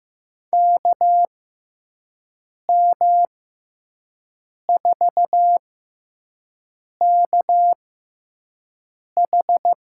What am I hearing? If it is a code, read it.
Morse code, heard as a keyed tone, about 15 wpm, spaced as for 5 wpm, 700 Hz. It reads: KM4KH
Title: USA Call Signs - 6 Characters - 6x Character Spacing - 15wpm